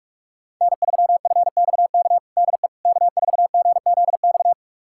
Morse code, heard as a keyed tone, about 45 wpm, 700 Hz. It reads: N3FXK DE K4GZX
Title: USA Callsign DE Callsign - 5 Characters - 45wpm